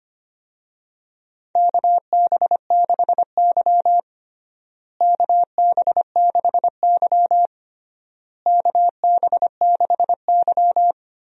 Morse code, heard as a keyed tone, about 25 wpm, 700 Hz. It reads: KB6Y KB6Y KB6Y
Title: USA Call Signs - 4 Characters - Repeated 3x - 25wpm